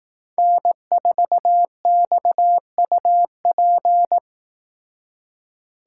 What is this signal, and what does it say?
Morse code, heard as a keyed tone, about 18 wpm, 700 Hz. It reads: N4XUP
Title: Speed Racing - USA 1x3 Call Signs 15wpm